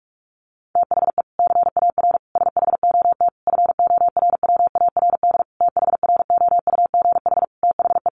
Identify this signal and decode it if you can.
Morse code, heard as a keyed tone, about 44 wpm, 700 Hz. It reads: THE CAR SHOT FORWARD THROUGH THE
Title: Sentences from Top 1300 Words 45wpm